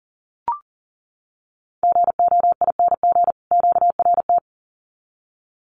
Morse code, heard as a keyed tone, about 40 wpm, 700 Hz. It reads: GOING QRT
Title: Common QSO Phrases 40wpm